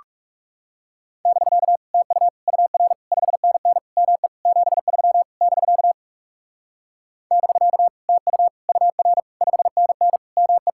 Morse code, heard as a keyed tone, about 37 wpm, 700 Hz. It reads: <BK> TU UR 5NN ME 73 <BK> <BK> TU UR 5NN ME
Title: Speed-Racing Common QSO Phrases Encore - 30wpm